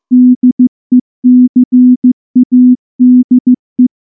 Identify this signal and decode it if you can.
Morse code, heard as a keyed tone, about 14 wpm, 250 Hz. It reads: DECADE
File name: morse / morse-training-15WPM